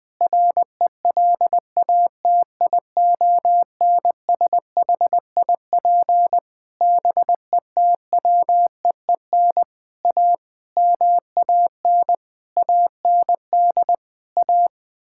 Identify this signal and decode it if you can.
Morse code, heard as a keyed tone, about 20 wpm, 700 Hz. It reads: RELATIONSHIP BETWEEN A MAN AND A
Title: Sentences with 6-Word Phrases - Spoken Beginning - 20wpm